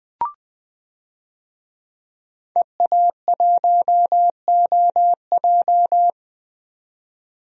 Morse code, heard as a keyed tone, about 20 wpm, 700 Hz. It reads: EA1OJ
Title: DXCC-100 Call Signs - 5 Characters - 20wpm